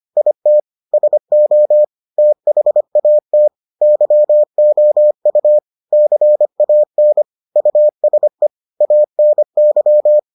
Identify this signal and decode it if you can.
Morse code, heard as a keyed tone, about 25 wpm, 600 Hz. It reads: IT SO THAT YOU CAN USE ANY